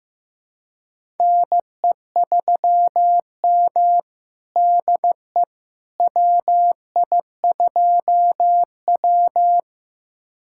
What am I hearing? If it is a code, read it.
Morse code, heard as a keyed tone, about 15 wpm, 700 Hz. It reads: NE3M DE WI2W
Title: USA Callsign DE Callsign - 4 Characters - 15wpm